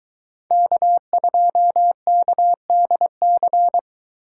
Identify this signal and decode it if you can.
Morse code, heard as a keyed tone, about 23 wpm, 700 Hz. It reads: K2KDC